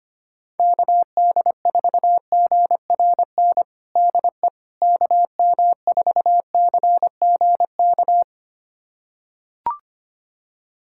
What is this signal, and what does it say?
Morse code, heard as a keyed tone, about 25 wpm, 700 Hz. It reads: KD4GRN DE KM4CGK
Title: USA Callsign DE Callsign - 6 Characters - 25wpm